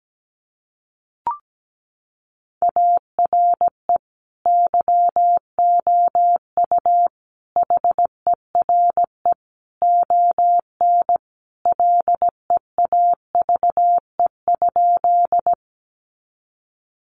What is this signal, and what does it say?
Morse code, heard as a keyed tone, about 17 wpm, 700 Hz. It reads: ARE YOU HERE ON LEAVE?
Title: Sentences from Top 400 Words 17wpm